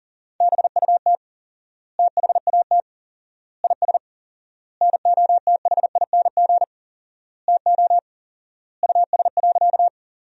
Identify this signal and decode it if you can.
Morse code, heard as a keyed tone, about 40 wpm, 700 Hz. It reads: BUT THAT IS NOTHING TO US.
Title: 4x Word-Spacing - Sentences from Top 500 Words - 40wpm